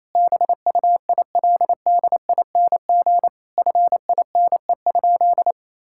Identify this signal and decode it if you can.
Morse code, heard as a keyed tone, about 28 wpm, 700 Hz. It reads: BUILDING FINE?
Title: Sentences from Top 1000 Words - Review Part 5 - 28wpm